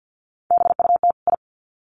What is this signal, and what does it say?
Morse code, heard as a keyed tone, about 50 wpm, 700 Hz. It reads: BUT I
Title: Sets of 2 Words 50wpm